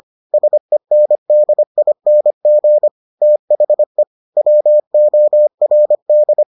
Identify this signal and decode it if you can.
Morse code, heard as a keyed tone, about 25 wpm, 600 Hz. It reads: SENDING THE WORD